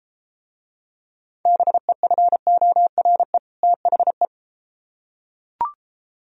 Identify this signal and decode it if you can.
Morse code, heard as a keyed tone, about 34 wpm, 700 Hz. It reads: BEFORE THE